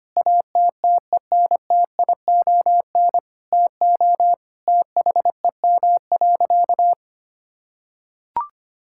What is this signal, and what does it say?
Morse code, heard as a keyed tone, about 25 wpm, 700 Hz. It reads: ATTENTION TO THEM.
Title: Sentences from Top 1000 Words - Review Part 6 - 25wpm